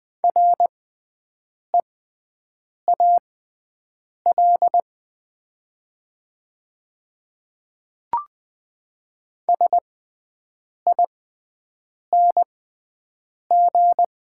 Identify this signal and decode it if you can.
Morse code, heard as a keyed tone, about 20 wpm, spaced as for 7 wpm, 700 Hz. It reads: REAL SING
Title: 6x Character Spacing - Top 500 Words - 20wpm